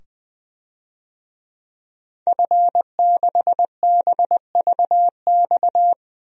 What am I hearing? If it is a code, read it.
Morse code, heard as a keyed tone, about 20 wpm, 700 Hz. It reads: F6BVX
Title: DXCC-100 Call Signs - 5 Characters - 20wpm